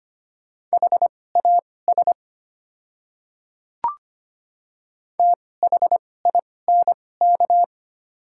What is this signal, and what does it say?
Morse code, heard as a keyed tone, about 25 wpm, spaced as for 18 wpm, 690 Hz. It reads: HAS THINK
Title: 2x Character Spacing - Top 200 Words - 25wpm